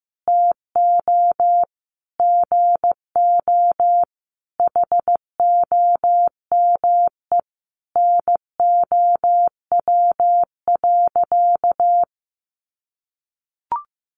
Morse code, heard as a keyed tone, about 15 wpm, 700 Hz. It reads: TO GO HOME NOW.
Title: Sentences from Top 1300 Words 15wpm